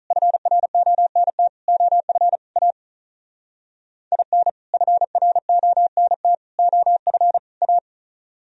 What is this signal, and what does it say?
Morse code, heard as a keyed tone, about 42 wpm, 700 Hz. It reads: FRONT OF A IN FRONT OF A
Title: Speed-Racing - Sets of 4 Words 35wpm